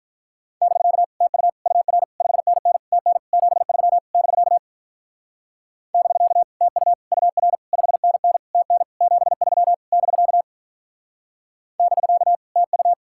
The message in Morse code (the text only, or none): <BK> TU UR 5NN TN 73 <BK> <BK> TU UR 5NN TN 73 <BK> <BK> TU